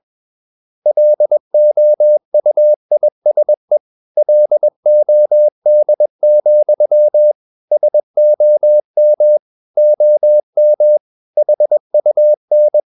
Morse code, heard as a keyed tone, about 21 wpm, 600 Hz. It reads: LOUISE LOD, SOM OM HUN
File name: Ferie QRS 21wpm